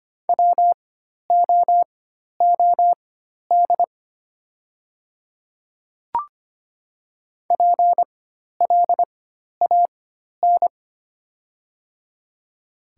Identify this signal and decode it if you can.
Morse code, heard as a keyed tone, about 25 wpm, spaced as for 12 wpm, 690 Hz. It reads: WOOD PLAN
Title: 4x Character Spacing - Top 500 Words - 25wpm